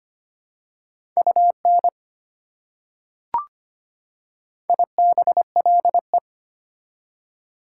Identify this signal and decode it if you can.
Morse code, heard as a keyed tone, about 25 wpm, 690 Hz. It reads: UN IBLE
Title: Prefixes and Suffixes 25wpm